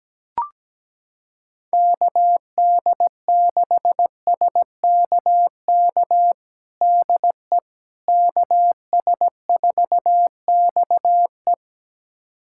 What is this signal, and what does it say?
Morse code, heard as a keyed tone, about 17 wpm, 700 Hz. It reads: KD6SKK DE KS4XE
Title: USA Callsign DE Callsign 17wpm